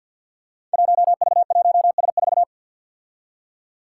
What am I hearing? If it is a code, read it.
Morse code, heard as a keyed tone, about 50 wpm, 700 Hz. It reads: 1V1S4
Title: Five Challenging Letter-Number Combinations 50wpm